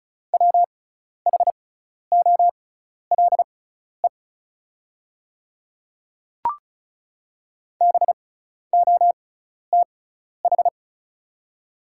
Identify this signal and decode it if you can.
Morse code, heard as a keyed tone, about 34 wpm, spaced as for 12 wpm, 690 Hz. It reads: WHOLE BOTH
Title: 6x Character Spacing - Top 500 Words - 35wpm